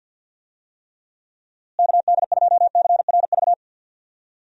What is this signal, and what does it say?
Morse code, heard as a keyed tone, about 47 wpm, 700 Hz. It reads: KD2CRV
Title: USA Call Signs - 6 Characters - Repeated 2x - 50wpm